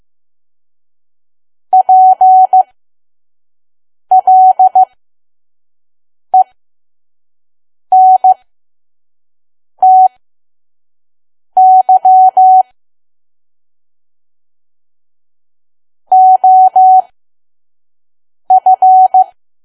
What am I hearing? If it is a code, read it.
Morse code, heard as a keyed tone, about 15 wpm, spaced as for 5 wpm, 750 Hz. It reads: PLENTY OF
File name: Archive / 5 WPM files / PM